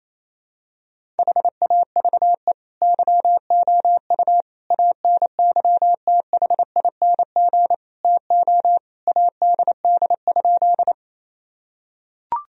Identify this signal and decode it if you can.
Morse code, heard as a keyed tone, about 28 wpm, 700 Hz. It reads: HAVE YOU ANYTHING TO ADD?
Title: Sentences from Top 1000 Words - Review Part 1 - 28wpm